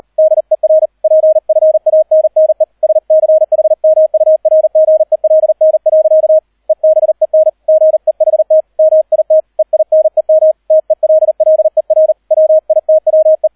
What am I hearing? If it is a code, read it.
Morse code, heard as a keyed tone, about 38 wpm, 630 Hz, with radio noise. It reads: DER PFANNE SCHMURGELN. EBEN GEHT MIT EINEM TELLER WITWE